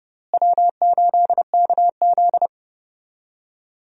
Morse code, heard as a keyed tone, about 30 wpm, 700 Hz. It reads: W8KZ
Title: USA Call Signs - 4 Characters - Repeated 2x - 30wpm